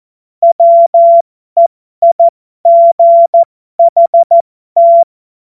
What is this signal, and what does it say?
Morse code, heard as a keyed tone, about 14 wpm, spaced as for 12 wpm, 680 Hz. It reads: WEIGHT
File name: Morsekurs / Witze